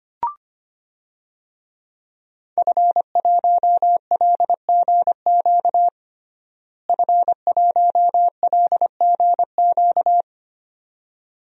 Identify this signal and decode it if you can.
Morse code, heard as a keyed tone, about 25 wpm, 700 Hz. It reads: F1LGQ F1LGQ
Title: DXCC-100 Call Signs - 5 Characters - Repeated 2x - 25wpm